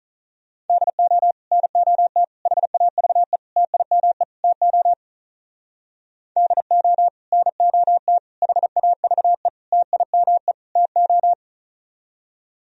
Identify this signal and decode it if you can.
Morse code, heard as a keyed tone, about 45 wpm, 700 Hz. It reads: DO NOT HAVE TIME TO DO NOT HAVE TIME TO